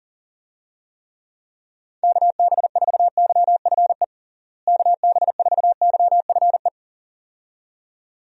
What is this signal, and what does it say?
Morse code, heard as a keyed tone, about 40 wpm, 700 Hz. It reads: KB4YFE KB4YFE
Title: USA Call Signs - 6 Characters - Repeated 2x - 40wpm